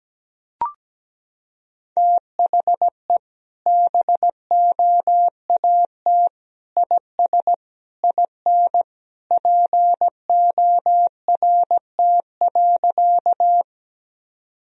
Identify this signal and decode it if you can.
Morse code, heard as a keyed tone, about 17 wpm, 700 Hz. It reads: THE BOAT IS IN PORT.
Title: Sentences from Top 1000 Words - Review Part 1 - 17wpm